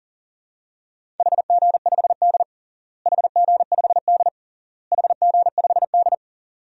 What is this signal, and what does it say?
Morse code, heard as a keyed tone, about 40 wpm, 700 Hz. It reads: HG5D HG5D HG5D
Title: DXCC-100 Call Signs - 4 Characters - Repeated 3x - 40wpm